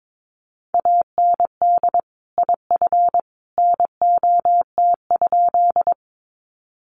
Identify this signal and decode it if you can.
Morse code, heard as a keyed tone, about 22 wpm, 700 Hz. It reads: AND IF NOT?
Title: Sentences from Top 600 Words 22wpm